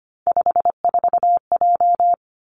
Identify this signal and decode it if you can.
Morse code, heard as a keyed tone, about 25 wpm, 700 Hz. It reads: 54J
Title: Three Challenging Letter-Number Combinations 25wpm